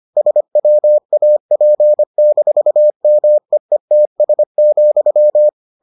Morse code, heard as a keyed tone, about 25 wpm, 600 Hz. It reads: SWAP-MEETS,